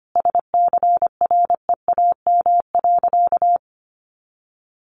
Morse code, heard as a keyed tone, about 25 wpm, 700 Hz. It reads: SCREAM.
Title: A Sound Of Thunder 25wpm